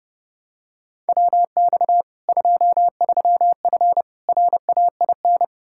Morse code, heard as a keyed tone, about 32 wpm, 700 Hz. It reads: WX 23F RAIN
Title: Practice QSO - Exchange 2 - Info - 30wpm